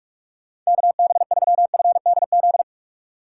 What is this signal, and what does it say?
Morse code, heard as a keyed tone, about 46 wpm, 700 Hz. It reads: KB3FDZ
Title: USA Call Signs - 6 Characters - 2x Character Spacing - 45wpm